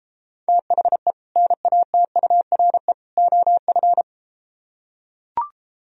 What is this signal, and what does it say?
Morse code, heard as a keyed tone, about 33 wpm, 700 Hz. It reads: THE NATURE OF